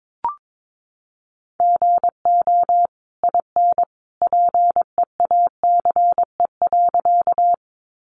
Morse code, heard as a keyed tone, about 22 wpm, 700 Hz. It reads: GO IN PEACE.
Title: Sentences from Top 1500 Words 22wpm